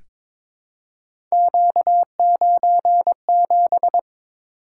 Morse code, heard as a keyed tone, about 22 wpm, 700 Hz. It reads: Q97
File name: practice / 22wpm / Three Letter-Number Combinations - 22wpm